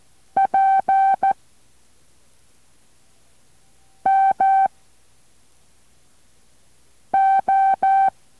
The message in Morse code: P M O